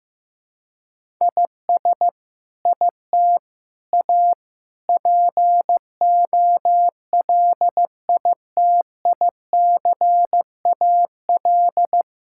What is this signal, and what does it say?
Morse code, heard as a keyed tone, about 15 wpm, 700 Hz. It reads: IS IT A POLITICAL
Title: Sentences from Top 1000 Words - Review Part 5 - 15wpm